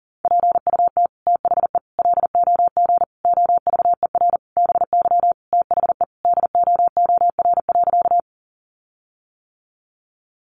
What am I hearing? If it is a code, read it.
Morse code, heard as a keyed tone, about 40 wpm, 700 Hz. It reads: PUT THE LOG OVER BY THE DOOR.
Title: Sentences from Top 1400 Words 40wpm